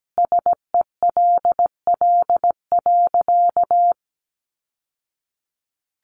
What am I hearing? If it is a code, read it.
Morse code, heard as a keyed tone, about 17 wpm, 700 Hz. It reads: SELL.
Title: Sentences from Top 1100 Words 17wpm